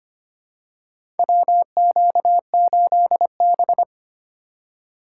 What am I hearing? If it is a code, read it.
Morse code, heard as a keyed tone, about 25 wpm, 700 Hz. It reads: WQ8B
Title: USA Call Signs - 2x Character Spacing - 25wpm